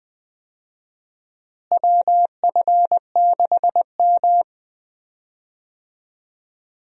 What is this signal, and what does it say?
Morse code, heard as a keyed tone, about 20 wpm, 700 Hz. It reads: WF6M
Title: USA Call Signs - 4 Characters - Repeated 3x - 20wpm